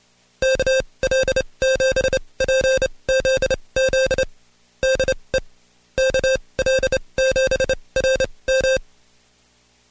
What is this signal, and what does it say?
Morse code, heard as a keyed tone, about 28 wpm, spaced as for 22 wpm, 530 Hz. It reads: KL7PZZ DE KL7RM
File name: Code Quick / AudioDownld2 / Seg137-Stepper10 21WPM